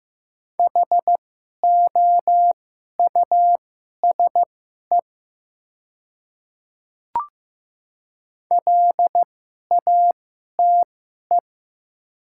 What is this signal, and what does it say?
Morse code, heard as a keyed tone, about 15 wpm, spaced as for 11 wpm, 700 Hz. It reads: HOUSE LATE
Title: 2x Character Spacing - Top 500 Words - 15wpm